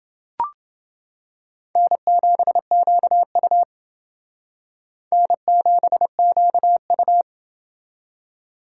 Morse code, heard as a keyed tone, about 29 wpm, 700 Hz. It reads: N7QU N7QU